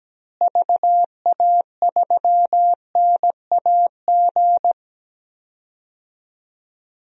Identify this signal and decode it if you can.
Morse code, heard as a keyed tone, about 17 wpm, 700 Hz. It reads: VA3NAG
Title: DXCC-100 Call Signs - 6 Characters - Repeated 3x - 17wpm